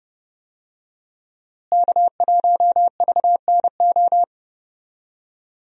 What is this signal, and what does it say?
Morse code, heard as a keyed tone, about 30 wpm, 700 Hz. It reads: K1VNO
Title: USA Call Signs - 5 Characters - Repeated 2x - 30wpm